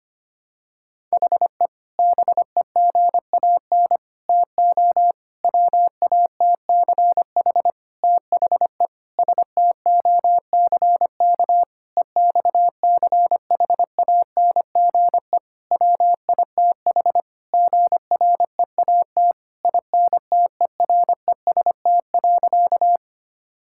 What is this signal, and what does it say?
Morse code, heard as a keyed tone, about 25 wpm, 700 Hz. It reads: HE BEGAN TO WATCH THE STOCK EXCHANGE WITH GREAT INTEREST.
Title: Sentences from Top 1000 Words - Review Part 6 - 25wpm